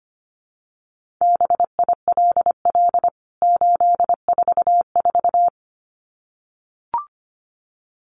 Morse code, heard as a keyed tone, about 25 wpm, 700 Hz. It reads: BILL 844
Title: CWT Exchange 25wpm